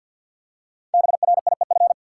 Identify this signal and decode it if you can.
Morse code, heard as a keyed tone, about 53 wpm, 700 Hz. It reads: BRIEF
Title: Five Letter Words - 50wpm